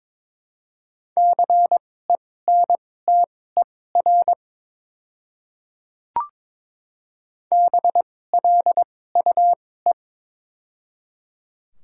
Morse code, heard as a keyed tone, about 23 wpm, spaced as for 16 wpm, 700 Hz. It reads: CENTER BLUE
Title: 2x Character Spacing - Top 500 Words - 22wpm